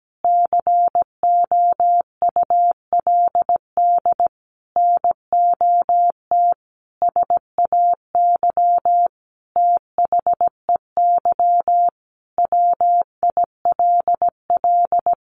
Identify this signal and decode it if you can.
Morse code, heard as a keyed tone, about 17 wpm, 700 Hz. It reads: COULD NOT SAY THEY WILL
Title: Sentences from Top 1100 Words 17wpm